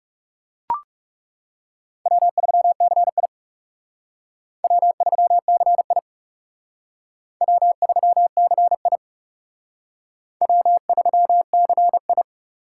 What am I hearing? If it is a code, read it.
Morse code, heard as a keyed tone, about 37 wpm, 700 Hz. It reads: W3CI W3CI W3CI W3CI
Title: Speed Racing - USA 1x2 Call Signs 30wpm